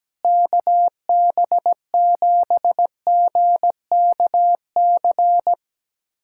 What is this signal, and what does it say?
Morse code, heard as a keyed tone, about 17 wpm, 700 Hz. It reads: KB7GKC
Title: USA Callsign DE Callsign - 6 Characters - 17wpm